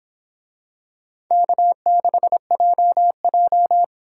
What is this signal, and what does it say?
Morse code, heard as a keyed tone, about 26 wpm, 700 Hz. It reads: K6JJ